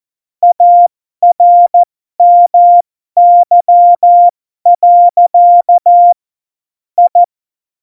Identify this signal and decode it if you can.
Morse code, heard as a keyed tone, about 14 wpm, spaced as for 12 wpm, 700 Hz. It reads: ARMY. I
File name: Morsekurs / Witze